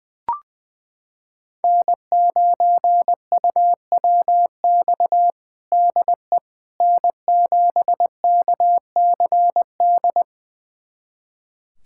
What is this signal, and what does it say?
Morse code, heard as a keyed tone, about 19 wpm, 700 Hz. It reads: N9UWX DE N7KCD
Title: USA Callsign DE Callsign - 5 Characters - 20wpm